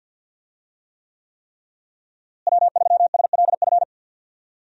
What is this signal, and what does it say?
Morse code, heard as a keyed tone, about 52 wpm, 700 Hz. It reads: W3SLF
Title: USA Call Signs - 5 Characters - Repeated 3x - 50wpm